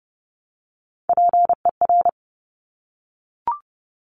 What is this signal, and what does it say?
Morse code, heard as a keyed tone, about 30 wpm, 700 Hz. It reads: PER
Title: Top Three Letter Combinations in Words 30wpm